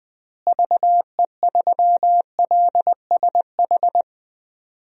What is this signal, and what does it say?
Morse code, heard as a keyed tone, about 20 wpm, 700 Hz. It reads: VE3LSH
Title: DXCC-100 Call Signs - 6 Characters - Repeated 3x - 20wpm